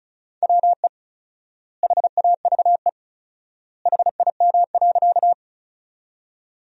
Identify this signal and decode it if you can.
Morse code, heard as a keyed tone, about 36 wpm, 700 Hz. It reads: WE HAVE HIM.
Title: 4x Word-Spacing - Sentences from Top 100 Words - 35wpm